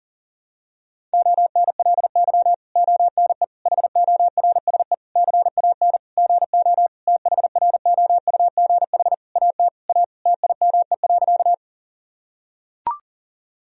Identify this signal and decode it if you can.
Morse code, heard as a keyed tone, about 37 wpm, 700 Hz. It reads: ONLY ONE HORSE CAN GO THROUGH AT A TIME.
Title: Sentences from Top 1000 Words - Review Part 1 - 40wpm